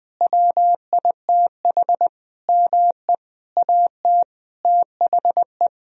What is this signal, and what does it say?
Morse code, heard as a keyed tone, about 20 wpm, 700 Hz. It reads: WITH ME AT THE
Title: Sentences from Top 1000 Words - Review Part 4 - 20wpm